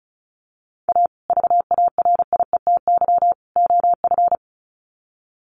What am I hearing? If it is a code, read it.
Morse code, heard as a keyed tone, about 34 wpm, 700 Hz. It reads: A VARIETY OF
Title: Sets of 3 Words 35wpm